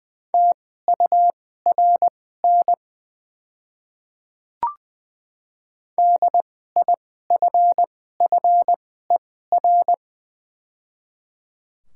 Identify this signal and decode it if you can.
Morse code, heard as a keyed tone, about 20 wpm, spaced as for 14 wpm, 700 Hz. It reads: TURN DIFFER
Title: 2x Character Spacing - Top 200 Words - 20wpm